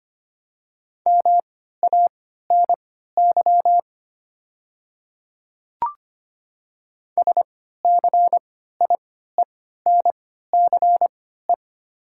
Morse code, heard as a keyed tone, about 25 wpm, spaced as for 14 wpm, 700 Hz. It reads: MANY SCIENCE
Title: 3x Character Spacing - Top 500 Words - 25wpm